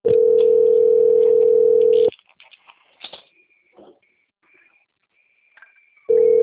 Ringback tone (North America), 440 + 480 Hz.